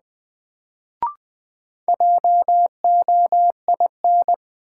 Morse code, heard as a keyed tone, about 20 wpm, 700 Hz. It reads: JOIN
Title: Sentences from Top 1000 Words - Review Part 6 - New Word Drill - 20wpm